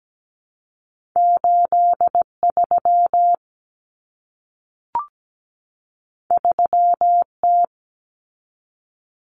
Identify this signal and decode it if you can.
Morse code, heard as a keyed tone, about 17 wpm, 700 Hz. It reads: 83 3T
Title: Common Cut Numbers - 2-Digit 17wpm